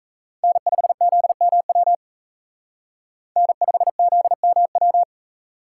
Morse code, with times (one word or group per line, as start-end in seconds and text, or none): 0.4-1.9 N5ZMW
3.4-5.0 N5ZMW